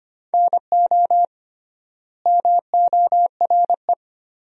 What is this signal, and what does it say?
Morse code, heard as a keyed tone, about 25 wpm, 700 Hz. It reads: NO MORE
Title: 3x Word-Spacing - Sentences from Top 400 Words - 25wpm